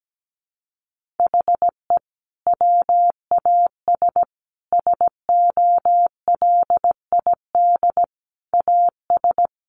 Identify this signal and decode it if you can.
Morse code, heard as a keyed tone, about 17 wpm, 700 Hz. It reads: HE WAS SOLID AS
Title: Sentences from Top 1500 Words 17wpm